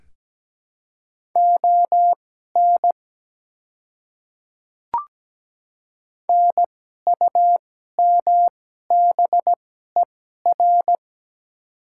Morse code, heard as a keyed tone, about 17 wpm, spaced as for 12 wpm, 700 Hz. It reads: ON NUMBER
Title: 2x Character Spacing - Top 100 Words - 17wpm